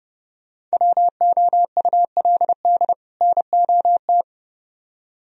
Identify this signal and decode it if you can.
Morse code, heard as a keyed tone, about 36 wpm, 700 Hz. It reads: WOULD NOT
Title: Speed-Racing - Sets of 2 Words 30wpm